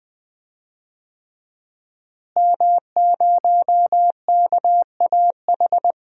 Morse code, heard as a keyed tone, about 20 wpm, 700 Hz. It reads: M0KAH
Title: DXCC-100 Call Signs - 5 Characters - Repeated 3x - 20wpm